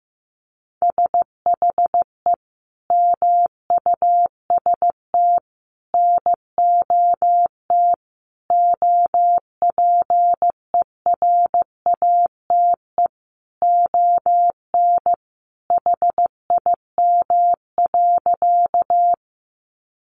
Morse code, heard as a keyed tone, about 15 wpm, 700 Hz. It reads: SHE MUST NOT OPERATE ON HIM.
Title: Sentences from Top 1500 Words 15wpm